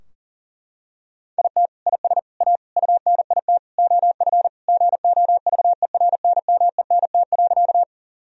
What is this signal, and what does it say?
Morse code, heard as a keyed tone, about 40 wpm, 700 Hz. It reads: IT IS A UNIT OF GOVERNMENT.
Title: Sentences from Top 1000 Words - Review Part 2 - 40wpm